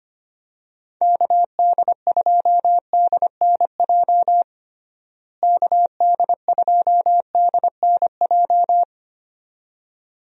Morse code, heard as a keyed tone, about 25 wpm, 700 Hz. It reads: KD2DNJ KD2DNJ
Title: USA Call Signs - 6 Characters - Repeated 2x - 25wpm